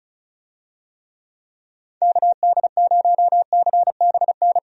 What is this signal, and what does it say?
Morse code, heard as a keyed tone, about 36 wpm, 700 Hz. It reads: KD0CBN
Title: USA Call Signs - 6 Characters - Repeated 3x - 35wpm